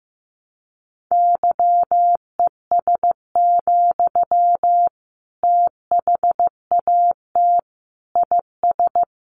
Morse code, heard as a keyed tone, about 15 wpm, 700 Hz. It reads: YES, THAT IS
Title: Sentences from Top 500 Words 15wpm